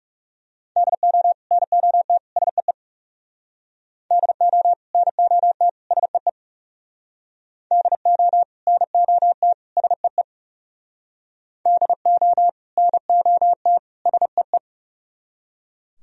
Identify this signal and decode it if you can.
Morse code, heard as a keyed tone, about 39 wpm, 700 Hz. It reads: DO NOT SEE DO NOT SEE DO NOT SEE DO NOT SEE